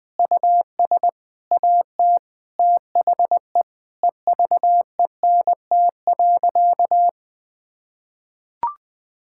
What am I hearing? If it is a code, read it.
Morse code, heard as a keyed tone, about 20 wpm, 700 Hz. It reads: US AT THE EVENT.
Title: Sentences with 5-Word Phrases - 20wpm